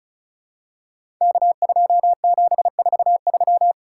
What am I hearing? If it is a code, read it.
Morse code, heard as a keyed tone, about 36 wpm, 700 Hz. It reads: K2743
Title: Code Groups - 5 Letter:Numbers - 35wpm